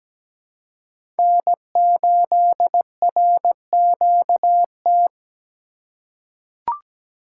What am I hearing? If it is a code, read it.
Morse code, heard as a keyed tone, about 17 wpm, 700 Hz. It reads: N8RQT
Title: USA Call Signs - 5 Characters - 4x Character Spacing - 17wpm